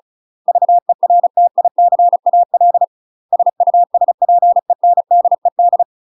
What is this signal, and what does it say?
Morse code, heard as a keyed tone, about 36 wpm, 700 Hz. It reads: VERTICAL SUSPENDED
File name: UK MASTER QSOs TRAINING / 35WPM / UK MASTER QSOs 0018